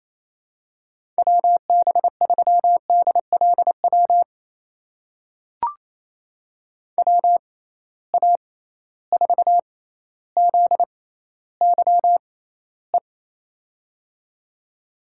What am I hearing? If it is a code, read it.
Morse code, heard as a keyed tone, about 28 wpm, spaced as for 14 wpm, 700 Hz. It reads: WB3DLW WA4ZYE
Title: USA Call Signs - 6 Characters - 6x Character Spacing - 28wpm